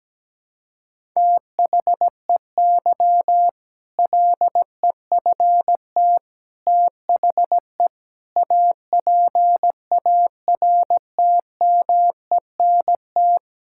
Morse code, heard as a keyed tone, about 17 wpm, 700 Hz. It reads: THEY LEFT THE APARTMENT
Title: Sentences from Top 1000 Words - Review Part 6 - 17wpm